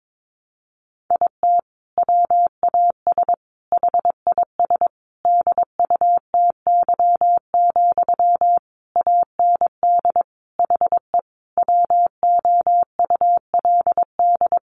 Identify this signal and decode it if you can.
Morse code, heard as a keyed tone, about 22 wpm, 700 Hz. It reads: IT WAS HIS DUTY, AND HE WOULD
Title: Sentences from Top 1500 Words 22wpm